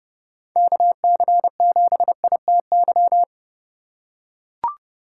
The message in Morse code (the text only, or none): KC7ITY